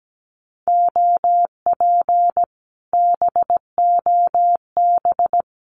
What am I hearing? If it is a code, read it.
Morse code, heard as a keyed tone, about 17 wpm, 700 Hz. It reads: OP BOB
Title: Common QSO Phrases Encore 17wpm